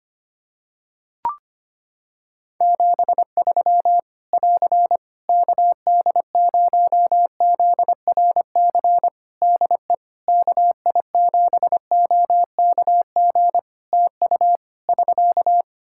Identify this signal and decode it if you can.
Morse code, heard as a keyed tone, about 25 wpm, 700 Hz. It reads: 73 <AR> KD0ZRC DE KI7OKG TU <SK>
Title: K1USN SST QSO Practice 25wpm